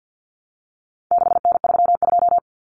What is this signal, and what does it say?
Morse code, heard as a keyed tone, about 48 wpm, 700 Hz. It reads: 6N32
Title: Common Cut Numbers - 4-Digit 50wpm